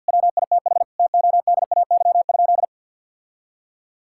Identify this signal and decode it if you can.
Morse code, heard as a keyed tone, about 50 wpm, 700 Hz. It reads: WITH TODAY?
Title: Sentences from Top 1000 Words - Review Part 6 - 50wpm